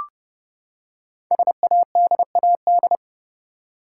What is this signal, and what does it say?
Morse code, heard as a keyed tone, about 30 wpm, 700 Hz. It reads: SADAD
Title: Code Groups - 5 Letters - 30wpm